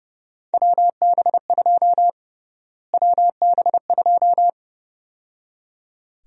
Morse code, heard as a keyed tone, about 30 wpm, 700 Hz. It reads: WB2 WB2
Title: USA Call Sign Prefixes - Repeated 2x - 30wpm